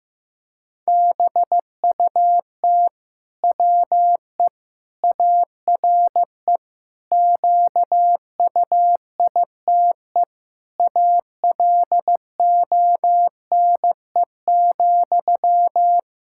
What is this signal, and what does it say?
Morse code, heard as a keyed tone, about 16 wpm, 700 Hz. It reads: BUT WE ARE QUITE ALONE,
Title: Sentences from Top 1600 Words 15wpm